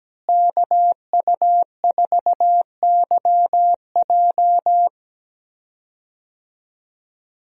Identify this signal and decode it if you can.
Morse code, heard as a keyed tone, about 17 wpm, 700 Hz. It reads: KU4YJ
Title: USA Call Signs - 5 Characters - Repeated 2x - 17wpm